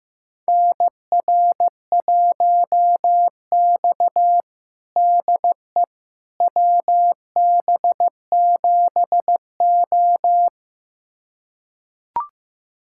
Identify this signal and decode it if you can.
Morse code, heard as a keyed tone, about 15 wpm, 700 Hz. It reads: NR1X DE WB7O
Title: USA Callsign DE Callsign - 4 Characters - 15wpm